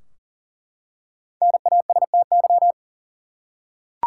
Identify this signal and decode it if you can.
Morse code, heard as a keyed tone, about 40 wpm, 700 Hz. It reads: NASTY
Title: Five Letter Words - 40wpm